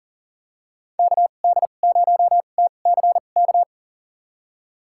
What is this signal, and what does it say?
Morse code, heard as a keyed tone, about 40 wpm, spaced as for 29 wpm, 700 Hz. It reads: KD0TCK
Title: USA Call Signs - 6 Characters - 2x Character Spacing - 40wpm